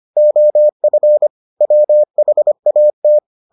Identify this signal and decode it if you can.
Morse code, heard as a keyed tone, about 25 wpm, 600 Hz. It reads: OF WHAT